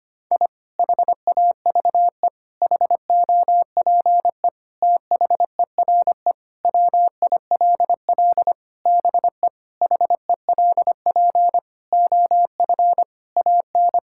Morse code, heard as a keyed tone, about 25 wpm, 700 Hz. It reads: I HAVE HOPE THERE WILL BE HELP OF AN
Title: Sentences from Top 1000 Words - Review Part 5 - 25wpm